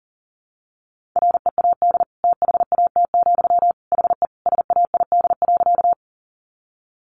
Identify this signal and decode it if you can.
Morse code, heard as a keyed tone, about 40 wpm, 700 Hz. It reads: READ THAT, HE SAID.
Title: Sentences from Top 200 Words 40wpm